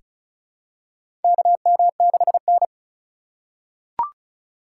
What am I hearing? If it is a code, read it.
Morse code, heard as a keyed tone, about 36 wpm, 700 Hz. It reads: KM6N
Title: USA Call Signs - 4 Characters - 3x Character Spacing - 35wpm